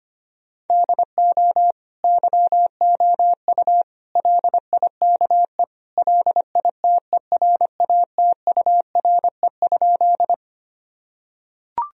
Morse code, heard as a keyed tone, about 25 wpm, 700 Hz. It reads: DO YOU LIKE LITERATURE?
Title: Sentences from Top 1000 Words - Review Part 3 - 25wpm